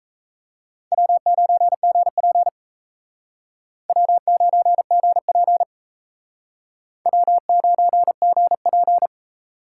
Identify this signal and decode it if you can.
Morse code, heard as a keyed tone, about 36 wpm, 700 Hz. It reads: W9GP W9GP W9GP